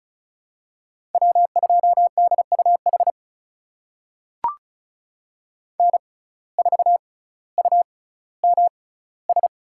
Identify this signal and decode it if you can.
Morse code, heard as a keyed tone, about 34 wpm, spaced as for 18 wpm, 700 Hz. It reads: W2DUH N4UMS
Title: USA Call Signs - 5 Characters - 6x Character Spacing - 35wpm